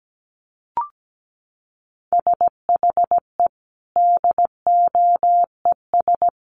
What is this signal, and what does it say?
Morse code, heard as a keyed tone, about 17 wpm, 700 Hz. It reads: SHE DOES
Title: Sentences from Top 1300 Words 17wpm